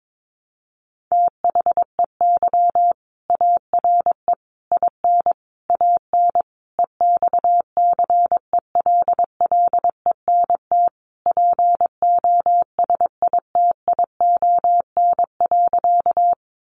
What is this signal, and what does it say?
Morse code, heard as a keyed tone, about 22 wpm, 700 Hz. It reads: THEY ARE IN AN EXCELLENT POSITION.
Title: Sentences from Top 1000 Words 22wpm